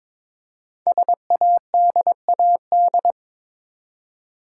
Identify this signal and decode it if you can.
Morse code, heard as a keyed tone, about 22 wpm, 700 Hz. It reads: SADAD
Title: Code Groups - 5 Letters - 22wpm